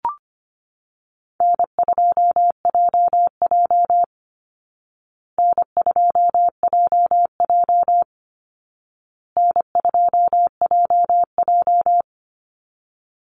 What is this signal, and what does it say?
Morse code, heard as a keyed tone, about 25 wpm, 700 Hz. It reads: N2JJ N2JJ N2JJ
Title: Easy Call Signs Repeated 3x - 25wpm